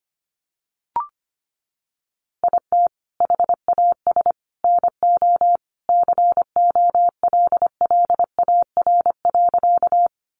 Morse code, heard as a keyed tone, about 25 wpm, 700 Hz. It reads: IT HAS NO COLLAR.
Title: Easy Phrases 25wpm